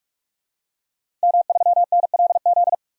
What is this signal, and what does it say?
Morse code, heard as a keyed tone, about 47 wpm, 700 Hz. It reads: M3NLZ
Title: DXCC-100 Call Signs - 5 Characters - 45wpm